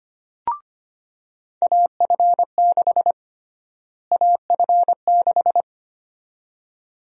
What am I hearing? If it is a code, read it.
Morse code, heard as a keyed tone, about 25 wpm, 700 Hz. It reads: AF6 AF6
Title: USA Call Sign Prefixes - Repeated 2x - 25wpm